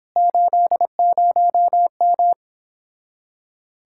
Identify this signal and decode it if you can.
Morse code, heard as a keyed tone, about 26 wpm, 700 Hz. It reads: 80M
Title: Speed-Racing Common QSO Phrases - 22wpm